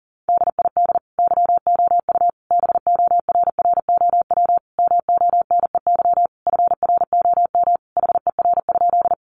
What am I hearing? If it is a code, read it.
Morse code, heard as a keyed tone, about 40 wpm, 700 Hz. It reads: DID YOU BORROW MONEY FROM HER?
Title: Sentences from Top 1500 Words 40wpm